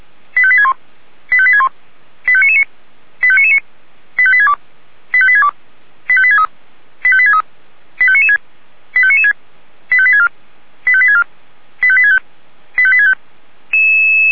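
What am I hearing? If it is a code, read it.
Five-tone selective call (ZVEI), heard as quick runs of five tones, tones of about 70 ms. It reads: 85861 85861 85999 85999 85862 85862 85863 85863 85996 85996 85864 85864 85865 85865
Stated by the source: ZVEI five-tone selective call 85745 received with an SDR.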